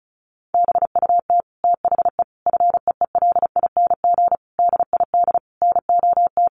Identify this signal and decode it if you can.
Morse code, heard as a keyed tone, about 34 wpm, 700 Hz. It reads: BUT THE FEELING DID NOT
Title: Sentences from Top 1200 Words 35wpm